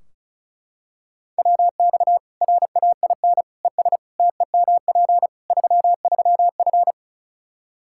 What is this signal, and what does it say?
Morse code, heard as a keyed tone, about 36 wpm, 700 Hz. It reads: WX RAIN ES TEMP 33F
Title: Practice QSO - Exchange 2 - Info - 35wpm